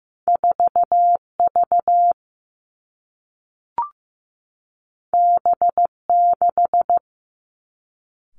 Morse code, heard as a keyed tone, about 15 wpm, 700 Hz. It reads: 4V B6
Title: Two Challenging Letter-Number Combinations 15wpm